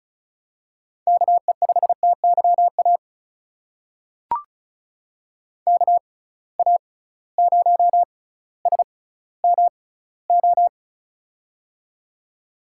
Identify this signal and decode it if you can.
Morse code, heard as a keyed tone, about 36 wpm, spaced as for 18 wpm, 700 Hz. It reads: KE5TYA KA0SMO
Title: USA Call Signs - 6 Characters - 6x Character Spacing - 35wpm